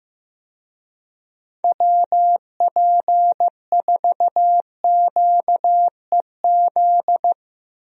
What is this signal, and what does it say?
Morse code, heard as a keyed tone, about 15 wpm, 700 Hz. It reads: WP4QEZ
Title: USA Call Signs - 6 Characters - Repeated 2x - 15wpm